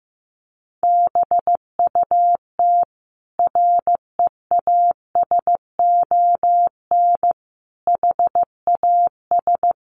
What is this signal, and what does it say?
Morse code, heard as a keyed tone, about 14 wpm, 700 Hz. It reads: BUT REASON HAS
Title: Sentences from Top 1100 Words 15wpm